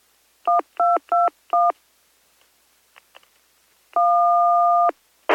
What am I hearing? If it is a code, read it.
DTMF, heard as telephone touch tones, tones of about 160 ms. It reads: 1221 1